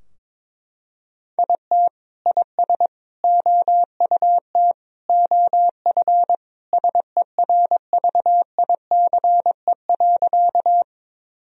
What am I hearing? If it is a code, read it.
Morse code, heard as a keyed tone, about 22 wpm, 700 Hz. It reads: IT IS OUT OF SERVICE.
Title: Sentences from Top 1000 Words - Review Part 4 - 22wpm